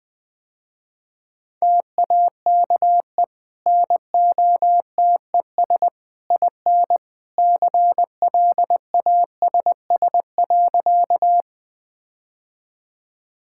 Morse code, heard as a keyed tone, about 20 wpm, 700 Hz. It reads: TAKE NOTES IN CLASS.
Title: Easy Sentences with Hints - 20wpm